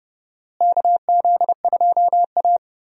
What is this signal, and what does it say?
Morse code, heard as a keyed tone, about 30 wpm, 700 Hz. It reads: KZ2A
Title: USA Call Signs - 4 Characters - Repeated 2x - 30wpm